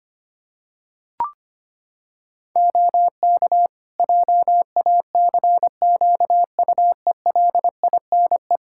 Morse code, heard as a keyed tone, about 25 wpm, 700 Hz. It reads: OK JACQUELINE
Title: K1USN SST QSO Practice 25wpm